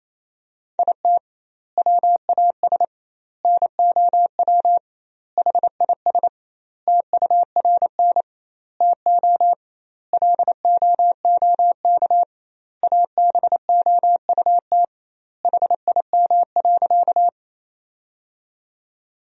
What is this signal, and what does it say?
Morse code, heard as a keyed tone, about 28 wpm, 700 Hz. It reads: IT WAS NOW HIS TURN TO LOOK ABOUT HIM.
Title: 2x Word-Spacing - Sentences from Top 200 Words - 28wpm